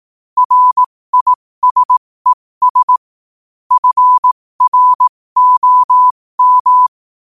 Morse code, heard as a keyed tone, about 18 wpm, spaced as for 15 wpm, 1000 Hz. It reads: RISES FROM
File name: qotdfiles / 15wpm